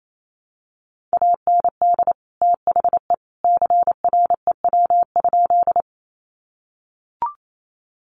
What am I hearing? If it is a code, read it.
Morse code, heard as a keyed tone, about 28 wpm, 700 Hz. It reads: AND THE CREW?
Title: Sentences from Top 1500 Words 28wpm